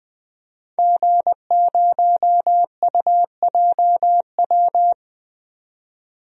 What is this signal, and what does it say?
Morse code, heard as a keyed tone, about 21 wpm, 700 Hz. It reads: G0UJW
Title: DXCC-100 Call Signs - 5 Characters - 20wpm